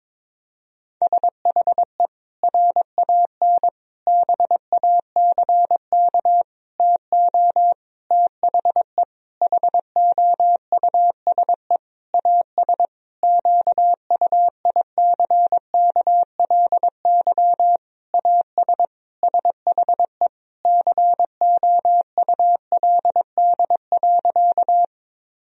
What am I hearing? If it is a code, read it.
Morse code, heard as a keyed tone, about 22 wpm, 700 Hz. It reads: SHE RAN BACK TO THE HOUSE AS QUICKLY AS SHE COULD.
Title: Sentences from Top 1000 Words - Review Part 3 - 22wpm